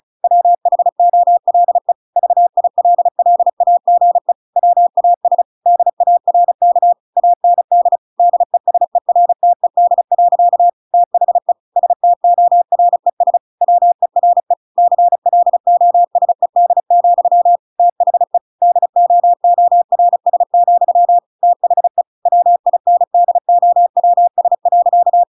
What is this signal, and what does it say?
Morse code, heard as a keyed tone, about 34 wpm, 700 Hz. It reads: WHOLE VILLAGE WAS DARK AND DESERTED. THE STORES WERE CLOSED, THE DOORS, THE WINDOWS.
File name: Pinocchio / Pinocchio ch6 35 35 ik1hge